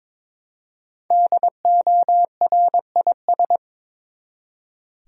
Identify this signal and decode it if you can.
Morse code, heard as a keyed tone, about 23 wpm, 700 Hz. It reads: DORIS
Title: USA Common Names - Max 5 Letters - 22wpm